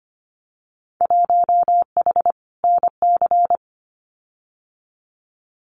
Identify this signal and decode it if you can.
Morse code, heard as a keyed tone, about 25 wpm, 700 Hz. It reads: 1H NC
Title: Winter Field Day Exchanges 25wpm